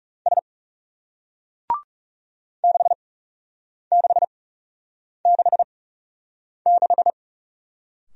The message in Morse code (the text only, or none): S 6666